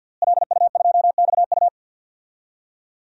Morse code, heard as a keyed tone, about 48 wpm, 700 Hz. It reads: LU2XU
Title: DXCC-100 Call Signs - 5 Characters - Repeated 2x - 50wpm